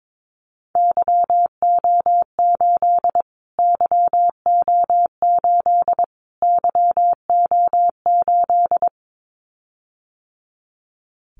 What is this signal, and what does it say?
Morse code, heard as a keyed tone, about 22 wpm, 700 Hz. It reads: YO8 YO8 YO8
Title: Callsign Prefixes for Top 25 DXCC - Repeated 3x - 22wpm